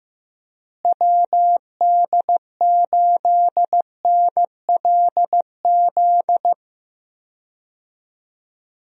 Morse code, heard as a keyed tone, about 16 wpm, 700 Hz. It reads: WD8NLZ
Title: USA Call Signs - 6 Characters - Repeated 2x - 15wpm